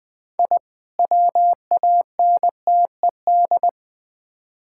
Morse code, heard as a keyed tone, about 20 wpm, 700 Hz. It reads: I WANTED
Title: Build A Phrase - 3-Word Phrases - 20wpm